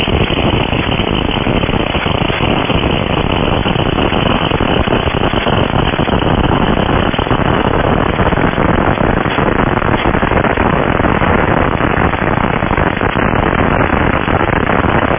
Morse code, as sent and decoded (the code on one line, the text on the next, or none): none